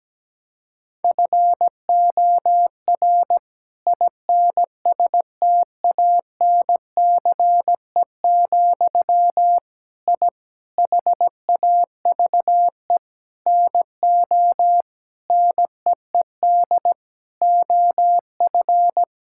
Morse code, heard as a keyed tone, about 17 wpm, 700 Hz. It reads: FOR INSTANCE, I HAVE NO NEED OF
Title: Sentences from Top 1000 Words - Review Part 5 - 17wpm